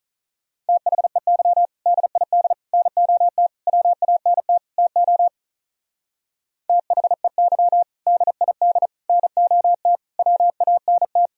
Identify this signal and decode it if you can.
Morse code, heard as a keyed tone, about 46 wpm, 700 Hz. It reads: THEY DID NOT WANT TO THEY DID NOT WANT